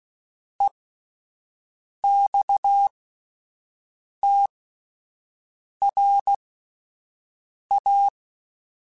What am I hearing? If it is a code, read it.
Morse code, heard as a keyed tone, about 16 wpm, spaced as for 5 wpm, 800 Hz. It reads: EXTRA
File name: QSOs Spaced